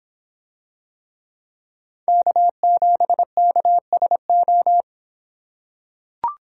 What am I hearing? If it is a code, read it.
Morse code, heard as a keyed tone, about 26 wpm, 700 Hz. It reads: K7KSO